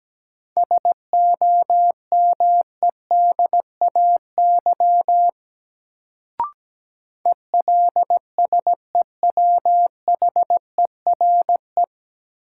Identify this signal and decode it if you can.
Morse code, heard as a keyed tone, about 17 wpm, 700 Hz. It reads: SOMEDAY ELSEWHERE
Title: Sentences from Top 1600 Words - New Word Drill Alt - 17wpm